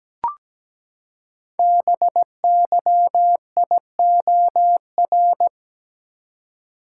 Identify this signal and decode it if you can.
Morse code, heard as a keyed tone, about 17 wpm, 700 Hz. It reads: BYIOR
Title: Code Groups - 5 Letters - 17wpm